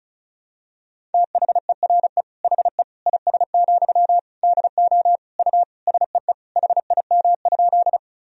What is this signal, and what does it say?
Morse code, heard as a keyed tone, about 36 wpm, 700 Hz. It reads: THERE HE IS, DO U SEE HIM?
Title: Sentences from Top 100 Words and Abbreviations 35wpm